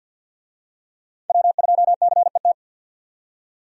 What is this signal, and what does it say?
Morse code, heard as a keyed tone, about 52 wpm, 700 Hz. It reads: W2CET
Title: USA Call Signs - 5 Characters - 50wpm